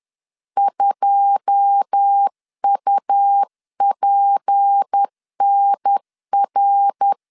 Morse code, heard as a keyed tone, about 10 wpm, 800 Hz. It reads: 2UPNR
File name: LEZIONI / 9 28 10wpm 2842